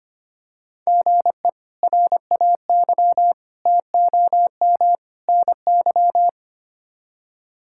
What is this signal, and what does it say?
Morse code, heard as a keyed tone, about 25 wpm, 700 Hz. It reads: GE RAY TOM NY
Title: K1USN SST QSO Practice - 25wpm